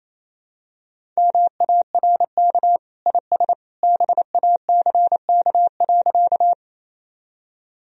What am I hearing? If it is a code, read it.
Morse code, heard as a keyed tone, about 28 wpm, 700 Hz. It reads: MARK IS BACK.
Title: Sentences from Top 1000 Words - Review Part 1 - 28wpm